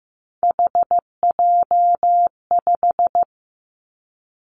Morse code, heard as a keyed tone, about 15 wpm, 700 Hz. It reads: HJ5
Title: Callsign Prefixes for Top 100 DXCC - 15wpm